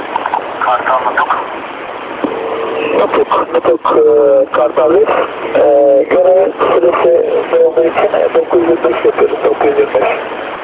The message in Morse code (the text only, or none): none